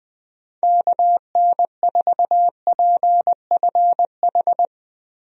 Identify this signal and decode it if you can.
Morse code, heard as a keyed tone, about 20 wpm, 700 Hz. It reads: KN4PFH
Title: USA Call Signs - 6 Characters - Repeated 3x - 20wpm